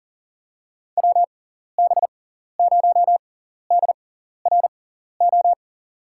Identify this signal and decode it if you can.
Morse code, heard as a keyed tone, about 40 wpm, spaced as for 14 wpm, 700 Hz. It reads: WB0DRO
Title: USA Call Signs - 6 Characters - 6x Character Spacing - 40wpm